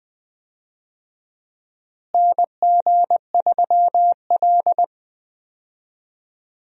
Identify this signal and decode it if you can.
Morse code, heard as a keyed tone, about 20 wpm, 700 Hz. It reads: NG3L
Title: USA Call Signs - 4 Characters - Repeated 2x - 20wpm